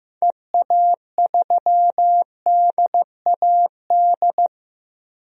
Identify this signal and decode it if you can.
Morse code, heard as a keyed tone, about 16 wpm, 700 Hz. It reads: EA3DAD
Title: DXCC-100 Call Signs - 6 Characters - Repeated 3x - 15wpm